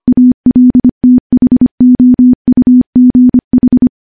Morse code, heard as a keyed tone, about 25 wpm, 250 Hz. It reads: ALTHOUGH
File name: morse-training-25WPM